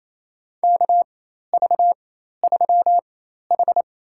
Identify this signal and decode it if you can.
Morse code, heard as a keyed tone, about 28 wpm, spaced as for 13 wpm, 700 Hz. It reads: KV3H
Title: USA Call Signs - 4x Character Spacing - 28wpm